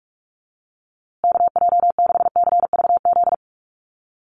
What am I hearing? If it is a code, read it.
Morse code, heard as a keyed tone, about 44 wpm, 700 Hz. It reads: KJ6CVZ
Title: USA Call Signs Repeated 2x 45wpm